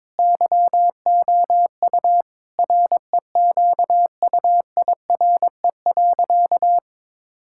Morse code, heard as a keyed tone, about 22 wpm, 700 Hz. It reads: YOU REQUIRE.
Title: Sentences from Top 1000 Words - Review Part 6 - 22wpm